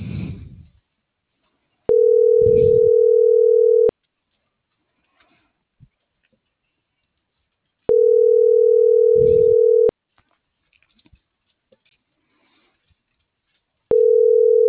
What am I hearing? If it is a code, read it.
Ringback tone (North America), 440 + 480 Hz.
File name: CENTRAL MULTIMODAL / out viridiana.juarez - .85084